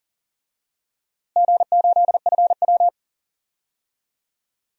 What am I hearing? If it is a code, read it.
Morse code, heard as a keyed tone, about 40 wpm, 700 Hz. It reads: G8FW
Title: DXCC-100 Call Signs - 4 Characters - Repeated 3x - 40wpm